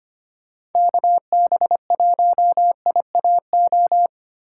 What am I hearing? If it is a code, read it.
Morse code, heard as a keyed tone, about 31 wpm, 700 Hz. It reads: KB1IAO
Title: Speed Racing - USA 2x3 Call Signs 25wpm